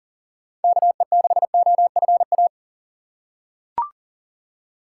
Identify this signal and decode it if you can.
Morse code, heard as a keyed tone, about 40 wpm, 700 Hz. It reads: KE6OFA
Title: USA Call Signs - 6 Characters - 4x Character Spacing - 40wpm